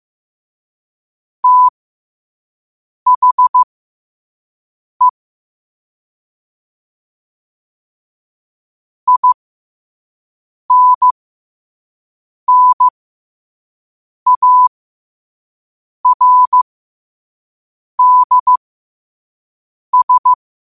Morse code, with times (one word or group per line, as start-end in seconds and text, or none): 1.4-5.1 THE
9.1-20.3 INNARDS